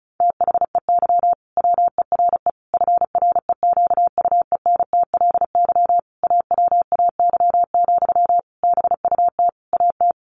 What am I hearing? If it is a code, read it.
Morse code, heard as a keyed tone, about 34 wpm, 700 Hz. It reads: THEY WERE FREQUENTLY AWAY, BUT AT
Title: Sentences from Top 1400 Words 35wpm